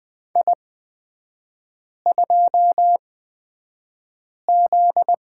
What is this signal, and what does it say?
Morse code, heard as a keyed tone, about 20 wpm, spaced as for 5 wpm, 700 Hz. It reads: I2Z
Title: 243 Lesson 39e - Letter z - Three Characters 20wpm